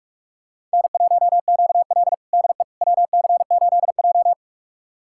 Morse code, heard as a keyed tone, about 46 wpm, 700 Hz. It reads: N1QL DE WC8J
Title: USA Callsign DE Callsign - 4 Characters - 45wpm